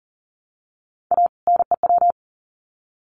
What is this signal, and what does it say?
Morse code, heard as a keyed tone, about 40 wpm, 700 Hz. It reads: A NEW
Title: Sets of 2 Words 40wpm